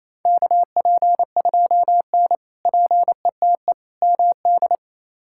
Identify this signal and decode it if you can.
Morse code, heard as a keyed tone, about 28 wpm, 700 Hz. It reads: KP2N PETE MD
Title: K1USN SST QSO Practice - 28wpm